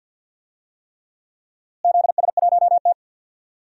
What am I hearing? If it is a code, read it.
Morse code, heard as a keyed tone, about 52 wpm, 700 Hz. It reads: ZS1T
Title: DXCC-100 Call Signs - 4 Characters - Repeated 3x - 50wpm